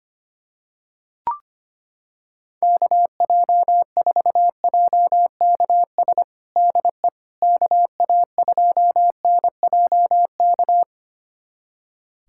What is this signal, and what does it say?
Morse code, heard as a keyed tone, about 25 wpm, 700 Hz. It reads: KJ4JKS DE KA2NJK
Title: USA Callsign DE Callsign - 6 Characters - 25wpm